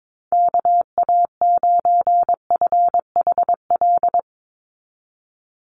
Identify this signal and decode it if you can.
Morse code, heard as a keyed tone, about 22 wpm, 700 Hz. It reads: KA9FHL
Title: USA Call Signs 22wpm